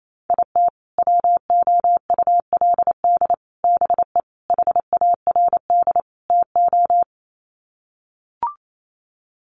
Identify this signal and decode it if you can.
Morse code, heard as a keyed tone, about 28 wpm, 700 Hz. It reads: IT WOULD BE HARD TO
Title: Sets of 5 Words Encore2 28wpm